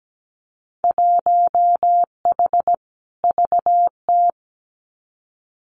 Morse code, heard as a keyed tone, about 17 wpm, 700 Hz. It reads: 1H VT
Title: Winter Field Day Exchanges 17wpm